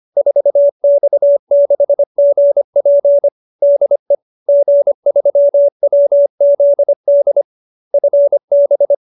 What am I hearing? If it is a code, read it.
Morse code, heard as a keyed tone, about 25 wpm, 590 Hz. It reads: 4X6GP DE G3WZD FB